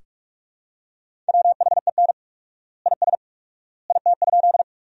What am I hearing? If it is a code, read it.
Morse code, heard as a keyed tone, about 46 wpm, 700 Hz. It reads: WHEN IS IT?
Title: 4x Word-Spacing - Sentences from Top 100 Words - 45wpm